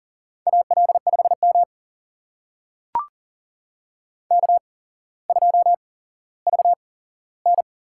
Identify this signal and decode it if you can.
Morse code, heard as a keyed tone, about 39 wpm, spaced as for 17 wpm, 700 Hz. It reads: AL5M K2VN
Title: USA Call Signs - 4 Characters - 8x Character Spacing - 40wpm